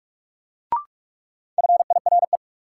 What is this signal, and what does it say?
Morse code, heard as a keyed tone, about 46 wpm, 700 Hz. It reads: FIRE
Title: Sentences from Top 400 Words - New Word Drill - 45wpm